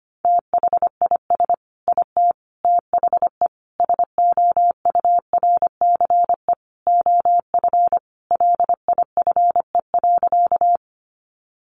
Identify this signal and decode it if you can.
Morse code, heard as a keyed tone, about 25 wpm, 700 Hz. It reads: THIS IT THE SOURCE OF LIFE.
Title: Sentences from Top 800 Words 25wpm